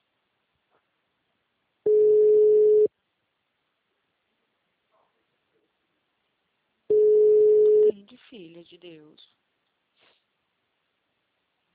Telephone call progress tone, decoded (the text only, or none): ringback tone